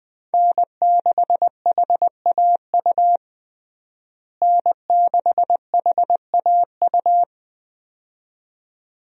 Morse code, heard as a keyed tone, about 21 wpm, 700 Hz. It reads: N6HAU N6HAU
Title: USA Call Signs - 5 Characters - Repeated 2x - 20wpm